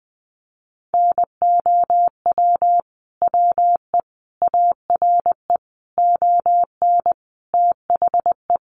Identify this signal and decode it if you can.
Morse code, heard as a keyed tone, about 20 wpm, 700 Hz. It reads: NOW WE ARE ON THE
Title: Sentences from Top 1200 Words 20wpm